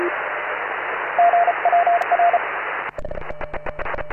Morse code, heard as a keyed tone, about 33 wpm, 700 Hz, with radio noise. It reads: GWR